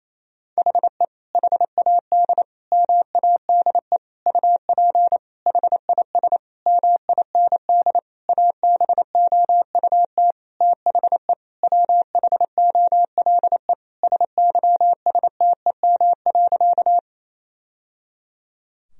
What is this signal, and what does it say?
Morse code, heard as a keyed tone, about 28 wpm, 700 Hz. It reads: HE HAD MADE UP HIS MIND ABOUT THE WHOLE SYSTEM.
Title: Sentences from Top 1000 Words - Review Part 3 - 28wpm